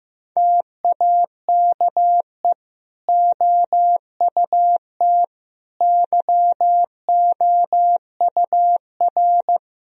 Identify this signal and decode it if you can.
Morse code, heard as a keyed tone, about 15 wpm, 700 Hz. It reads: TAKE OUT YOUR
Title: Sentences from Top 1000 Words - Review Part 2 - 15wpm